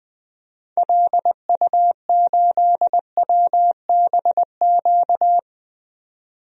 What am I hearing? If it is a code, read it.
Morse code, heard as a keyed tone, about 20 wpm, 700 Hz. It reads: LU8WBQ
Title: DXCC-100 Call Signs - 6 Characters - 20wpm